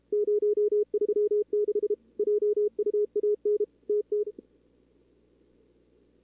Morse code, heard as a keyed tone, about 30 wpm, 440 Hz. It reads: 036 JUAN TT
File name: CWSprint / Feb2022 / k5tr-feb-na-sprint-cw-k5ot -2300